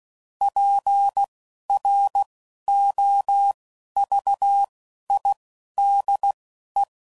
Morse code, heard as a keyed tone, about 16 wpm, spaced as for 11 wpm, 800 Hz. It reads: PROVIDE
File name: CW Code Groups @10WPM / Technical Subjects / SwPwrSup